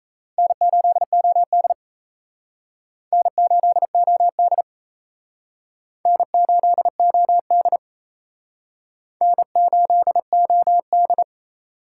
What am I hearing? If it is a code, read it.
Morse code, heard as a keyed tone, about 36 wpm, 700 Hz. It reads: N8OD N8OD N8OD N8OD